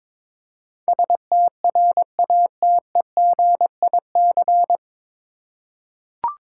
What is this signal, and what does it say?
Morse code, heard as a keyed tone, about 22 wpm, 700 Hz. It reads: STRATEGIC
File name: 9 Letter Words - First 3 Letters Given - 22wpm